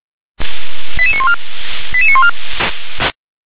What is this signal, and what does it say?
Five-tone selective call (ZVEI), heard as quick runs of five tones, tones of about 80 ms. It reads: 88914 88914